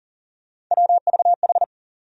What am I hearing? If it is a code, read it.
Morse code, heard as a keyed tone, about 40 wpm, 700 Hz. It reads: WVH
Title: USA Call Sign Postfixes - 3 Letters 40wpm